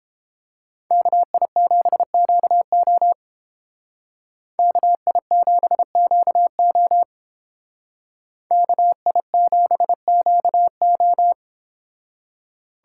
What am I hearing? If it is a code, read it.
Morse code, heard as a keyed tone, about 30 wpm, 700 Hz. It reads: KI7QO KI7QO KI7QO